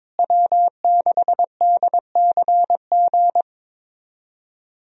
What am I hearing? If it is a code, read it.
Morse code, heard as a keyed tone, about 22 wpm, 700 Hz. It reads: W6DCG
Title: USA Call Signs - 5 Characters - Repeated 2x - 22wpm